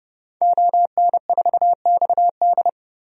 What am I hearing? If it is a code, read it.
Morse code, heard as a keyed tone, about 32 wpm, 700 Hz. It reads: ON4XD
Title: DXCC-100 Call Signs - 5 Characters - 30wpm